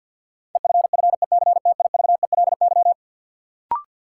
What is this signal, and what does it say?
Morse code, heard as a keyed tone, about 52 wpm, 700 Hz. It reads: EFFECTIVELY
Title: Sentences from Top 1400 Words - New Word Drill Alt - 50wpm